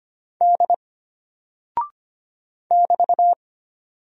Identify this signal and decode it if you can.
Morse code, heard as a keyed tone, about 25 wpm, 700 Hz. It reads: D <BT>
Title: Single Character (Letter:Number:Punctuation:Prosign) Alt 25wpm